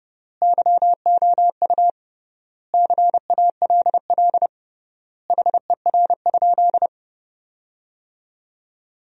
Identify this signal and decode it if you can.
Morse code, heard as a keyed tone, about 30 wpm, 700 Hz. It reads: YOU CALL HER?
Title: 3x Word-Spacing - Sentences from Top 100 Words - 30wpm